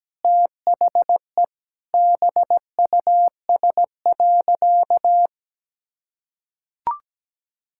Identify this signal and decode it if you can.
Morse code, heard as a keyed tone, about 17 wpm, 700 Hz. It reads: THE BUS.
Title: Sentences from Top 1000 Words - Review Part 6 - 17wpm